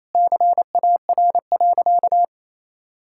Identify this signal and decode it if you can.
Morse code, heard as a keyed tone, about 28 wpm, 700 Hz. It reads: CAR.
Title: Sentences from Top 1000 Words - Review Part 5 - 28wpm